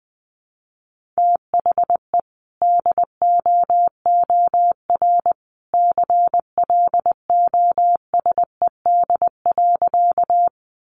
Morse code, heard as a keyed tone, about 20 wpm, 700 Hz. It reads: THE DOOR CLOSED.
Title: Sentences from Top 1400 Words 20wpm